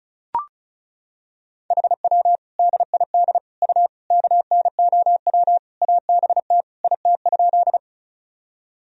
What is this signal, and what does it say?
Morse code, heard as a keyed tone, about 34 wpm, 700 Hz. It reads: HW DID U KNOW ABT IT?
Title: Sentences from Top 100 Words and Abbreviations 35wpm